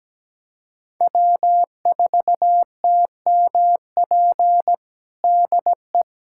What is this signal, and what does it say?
Morse code, heard as a keyed tone, about 17 wpm, 700 Hz. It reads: W4TMP DE
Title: USA Callsign DE Callsign - 5 Characters - 17wpm